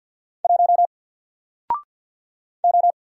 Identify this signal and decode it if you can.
Morse code, heard as a keyed tone, about 49 wpm, 700 Hz. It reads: 1 O
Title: Single Letter-Number Alt 50wpm